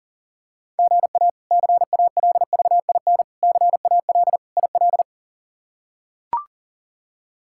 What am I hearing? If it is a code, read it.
Morse code, heard as a keyed tone, about 37 wpm, 700 Hz. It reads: GA CALVIN CAL IL
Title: K1USN SST QSO Practice - 40wpm